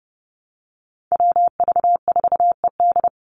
Morse code, heard as a keyed tone, about 30 wpm, 700 Hz. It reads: WV4ED
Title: USA Call Signs 30wpm